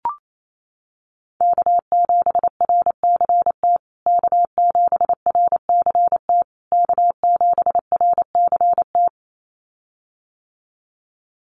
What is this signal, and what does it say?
Morse code, heard as a keyed tone, about 28 wpm, 700 Hz. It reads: K7RCT K7RCT K7RCT
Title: USA Call Signs Repeated 3x 28wpm